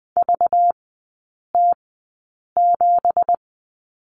Morse code, heard as a keyed tone, about 20 wpm, spaced as for 8 wpm, 700 Hz. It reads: VT7
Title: 205 Lesson 33e - Letter j - Three Characters 20wpm